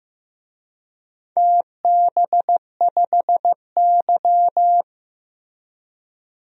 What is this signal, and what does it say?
Morse code, heard as a keyed tone, about 15 wpm, 700 Hz. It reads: TB5Y
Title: DXCC-100 Call Signs - 4 Characters - Repeated 3x - 15wpm